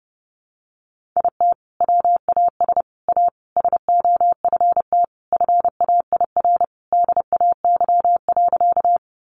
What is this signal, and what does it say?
Morse code, heard as a keyed tone, about 30 wpm, 700 Hz. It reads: IT WAS A SOFT FAIR DAY.
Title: Sentences from Top 1500 Words 30wpm